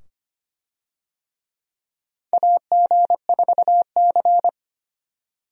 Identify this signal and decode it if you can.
Morse code, heard as a keyed tone, about 25 wpm, 700 Hz. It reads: AG4C
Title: USA Call Signs - 4 Characters - 25wpm